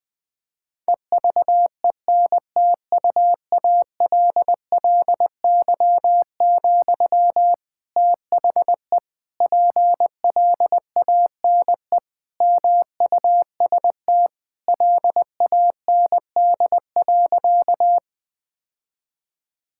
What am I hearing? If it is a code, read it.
Morse code, heard as a keyed tone, about 20 wpm, 700 Hz. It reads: EVENTUALLY, THE PLANE MUST LAND.
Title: Sentences from Top 1000 Words - Review Part 6 - 20wpm